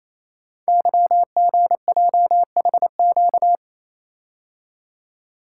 Morse code, heard as a keyed tone, about 28 wpm, 700 Hz. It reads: YGJHQ
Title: Code Groups - 5 Letters - 28wpm